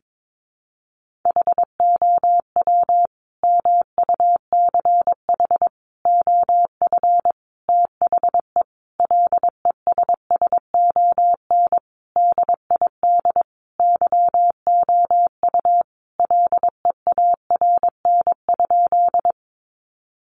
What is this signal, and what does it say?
Morse code, heard as a keyed tone, about 22 wpm, 700 Hz. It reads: HOW MUCH OF THE LESSON DID YOU LEARN?
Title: Sentences from Top 1200 Words 22wpm